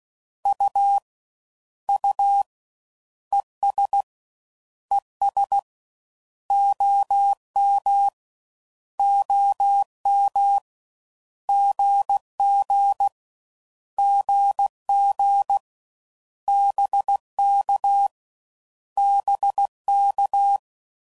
Morse code, heard as a keyed tone, about 16 wpm, 800 Hz. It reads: U U ES ES OM OM GG GG BK BK
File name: Mixed / Abreviations / Abreviations